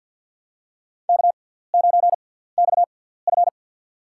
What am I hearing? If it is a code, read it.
Morse code, heard as a keyed tone, about 47 wpm, spaced as for 17 wpm, 700 Hz. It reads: K9XF
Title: USA Call Signs - 4 Characters - 6x Character Spacing - 50wpm